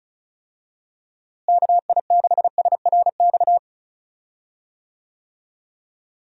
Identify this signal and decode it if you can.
Morse code, heard as a keyed tone, about 34 wpm, 700 Hz. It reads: KI6SRX
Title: USA Call Signs - 6 Characters - Repeated 2x - 35wpm